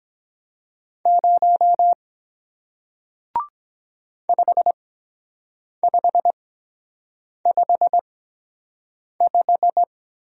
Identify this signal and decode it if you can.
Morse code, heard as a keyed tone, about 23 wpm, spaced as for 7 wpm, 700 Hz. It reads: 0 5555